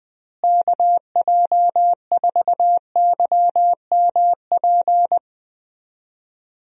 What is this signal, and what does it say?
Morse code, heard as a keyed tone, about 20 wpm, 700 Hz. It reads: KJ4YMP
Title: USA Call Signs - 6 Characters - Repeated 3x - 20wpm